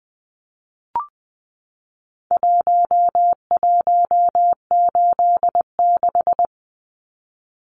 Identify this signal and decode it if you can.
Morse code, heard as a keyed tone, about 20 wpm, 700 Hz. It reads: 1186
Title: Numbers - Contest Style - 4-Digit 20wpm